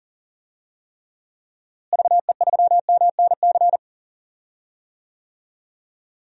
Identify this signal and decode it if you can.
Morse code, heard as a keyed tone, about 40 wpm, 700 Hz. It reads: VE3MNC
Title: DXCC-100 Call Signs - 6 Characters - Repeated 3x - 40wpm